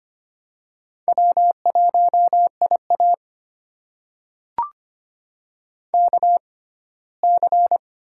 Morse code, heard as a keyed tone, about 25 wpm, spaced as for 17 wpm, 700 Hz. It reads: W1IA KC
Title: USA Call Signs - 6x Character Spacing - 25wpm